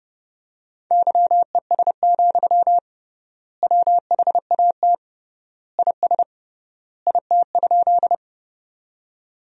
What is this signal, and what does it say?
Morse code, heard as a keyed tone, about 30 wpm, 700 Hz. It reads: YES, WHAT IS IT?
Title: 3x Word-Spacing - Sentences from Top 500 Words - 30wpm